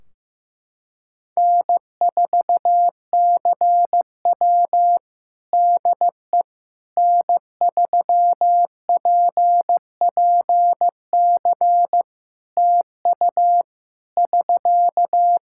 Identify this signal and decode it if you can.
Morse code, heard as a keyed tone, about 15 wpm, 700 Hz. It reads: N4CW DE N3PPC TU <SK>
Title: Practice QSO - Exchange 4 - Ending - 15wpm